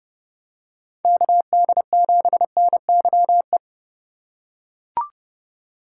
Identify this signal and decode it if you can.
Morse code, heard as a keyed tone, about 31 wpm, 700 Hz. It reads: KD7NYE
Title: USA Call Signs - 6 Characters - 2x Character Spacing - 30wpm